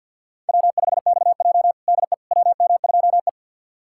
Morse code, heard as a keyed tone, about 52 wpm, 700 Hz. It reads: W5XJ DE WM2E
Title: USA Callsign DE Callsign - 4 Characters - 50wpm